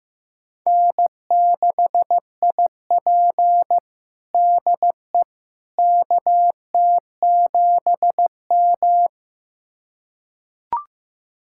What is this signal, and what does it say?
Morse code, heard as a keyed tone, about 15 wpm, 700 Hz. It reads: N6IP DE KT7M
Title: USA Callsign DE Callsign - 4 Characters - 15wpm